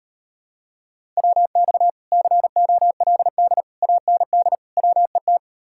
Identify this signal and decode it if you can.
Morse code, heard as a keyed tone, about 31 wpm, 700 Hz. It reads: WX COLD AND WET
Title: Speed-Racing Common QSO Phrases - 25wpm